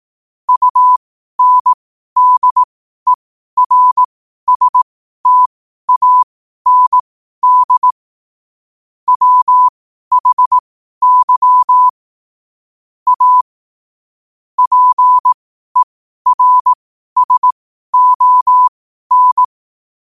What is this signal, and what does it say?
Morse code, heard as a keyed tone, about 18 wpm, spaced as for 12 wpm, 1000 Hz. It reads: UNDERSTAND WHY A PERSON
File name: qotdfiles / 12wpm